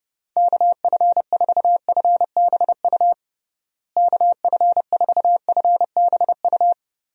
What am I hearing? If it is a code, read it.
Morse code, heard as a keyed tone, about 32 wpm, 700 Hz. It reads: KF4FBU KF4FBU
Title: USA Call Signs - 6 Characters - Repeated 2x - 30wpm